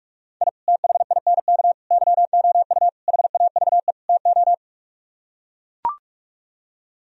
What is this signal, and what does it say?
Morse code, heard as a keyed tone, about 49 wpm, 700 Hz. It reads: I THINK YOU HAVE TO